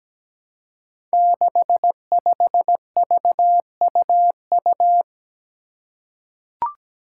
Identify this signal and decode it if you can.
Morse code, heard as a keyed tone, about 17 wpm, 700 Hz. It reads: 65VUU
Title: Five Challenging Letter-Number Combinations 17wpm